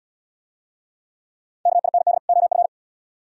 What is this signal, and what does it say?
Morse code, heard as a keyed tone, about 66 wpm, 700 Hz. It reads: KIND OF